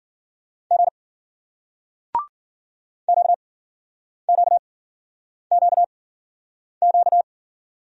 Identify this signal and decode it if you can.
Morse code, heard as a keyed tone, about 51 wpm, spaced as for 9 wpm, 700 Hz. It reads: G QQQQ